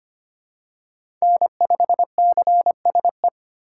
Morse code, heard as a keyed tone, about 25 wpm, 700 Hz. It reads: N5CSE
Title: USA Call Signs - 5 Characters - 8x Character Spacing - 25wpm